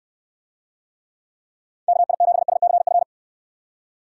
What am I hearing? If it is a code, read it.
Morse code, heard as a keyed tone, about 69 wpm, 700 Hz. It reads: KI7SCF